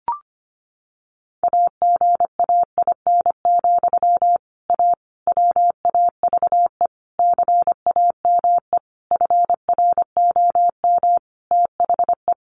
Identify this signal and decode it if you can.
Morse code, heard as a keyed tone, about 25 wpm, 700 Hz. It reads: AGAIN, A WAVE CAME FROM THE
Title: Sentences from Top 1400 Words 25wpm